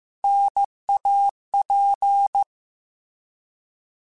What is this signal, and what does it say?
Morse code, heard as a keyed tone, about 15 wpm, 800 Hz. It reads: NAP
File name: Contrasts / Contrasts-mp3s / 3LtrContrasts @13